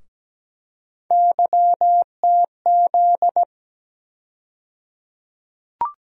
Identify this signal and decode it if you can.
Morse code, heard as a keyed tone, about 17 wpm, 700 Hz. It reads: YTZ
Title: USA Call Sign Postfixes - 3 Letters 17wpm